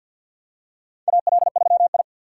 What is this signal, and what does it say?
Morse code, heard as a keyed tone, about 51 wpm, 700 Hz. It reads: AL3I
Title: USA Call Signs - 4 Characters - 3x Character Spacing - 50wpm